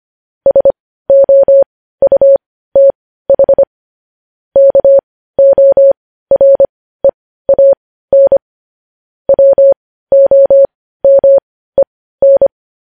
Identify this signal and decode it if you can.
Morse code, heard as a keyed tone, about 25 wpm, spaced as for 15 wpm, 550 Hz. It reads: SOUTH KOREAN WOMEN